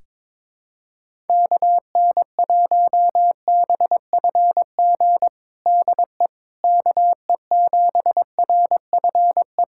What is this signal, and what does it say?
Morse code, heard as a keyed tone, about 22 wpm, 700 Hz. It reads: KN1BFG DE KE7RFE
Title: USA Callsign DE Callsign - 6 Characters - 22wpm